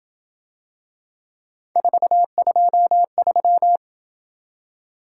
Morse code, heard as a keyed tone, about 27 wpm, 700 Hz. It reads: 423